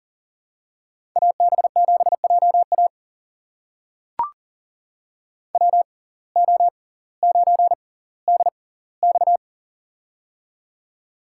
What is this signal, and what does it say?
Morse code, heard as a keyed tone, about 39 wpm, spaced as for 20 wpm, 700 Hz. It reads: AB7JA WO9DX
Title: USA Call Signs - 5 Characters - 6x Character Spacing - 40wpm